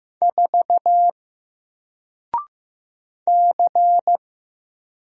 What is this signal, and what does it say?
Morse code, heard as a keyed tone, about 15 wpm, 700 Hz. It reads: 4 C
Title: Single Character (Letter:Number:Punctuation:Prosign) 15wpm